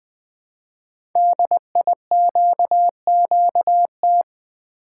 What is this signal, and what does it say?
Morse code, heard as a keyed tone, about 20 wpm, 700 Hz. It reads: DIQQT
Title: Code Groups - 5 Letters - 20wpm